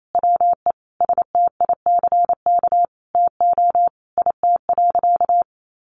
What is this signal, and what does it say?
Morse code, heard as a keyed tone, about 28 wpm, 700 Hz. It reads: WE STICK TO IT.
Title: Sentences from Top 1200 Words 28wpm